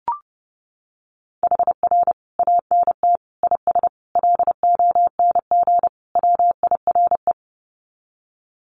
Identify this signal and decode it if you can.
Morse code, heard as a keyed tone, about 30 wpm, 700 Hz. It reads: HR ANT IS LONG WIRE
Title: Common QSO Phrases 30wpm